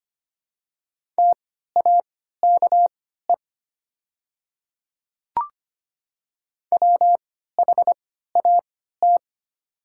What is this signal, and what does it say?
Morse code, heard as a keyed tone, about 25 wpm, spaced as for 14 wpm, 700 Hz. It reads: TAKE WHAT
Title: 3x Character Spacing - Top 300 Words - 25wpm